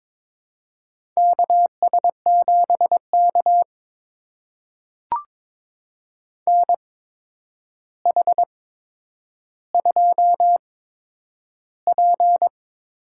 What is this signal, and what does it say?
Morse code, heard as a keyed tone, about 22 wpm, spaced as for 9 wpm, 700 Hz. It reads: KS7K NH2P
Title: USA Call Signs - 8x Character Spacing - 22wpm